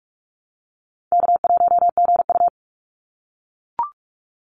Morse code, heard as a keyed tone, about 44 wpm, 700 Hz. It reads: K1GU
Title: Call Signs - 45wpm